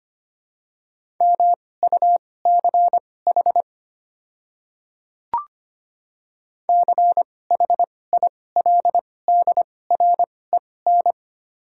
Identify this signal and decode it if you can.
Morse code, heard as a keyed tone, about 25 wpm, spaced as for 18 wpm, 700 Hz. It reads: MUCH CHILDREN
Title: 2x Character Spacing - Top 300 Words - 25wpm